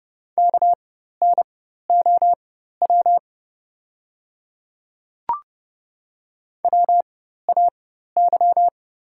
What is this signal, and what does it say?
Morse code, heard as a keyed tone, about 29 wpm, spaced as for 14 wpm, 700 Hz. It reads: KNOW WAY
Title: 4x Character Spacing - Top 100 Words - 30wpm